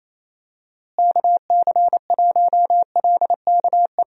Morse code, heard as a keyed tone, about 27 wpm, 700 Hz. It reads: KC1LKE
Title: USA Call Signs - 6 Characters - Repeated 3x - 28wpm